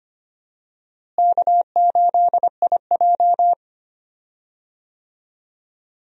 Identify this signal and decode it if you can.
Morse code, heard as a keyed tone, about 25 wpm, 700 Hz. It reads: K8IJ
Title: USA Call Signs - 4 Characters - Repeated 3x - 25wpm